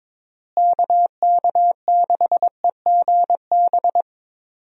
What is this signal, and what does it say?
Morse code, heard as a keyed tone, about 22 wpm, 700 Hz. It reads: KK6EGB
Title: USA Call Signs - 6 Characters - Repeated 3x - 22wpm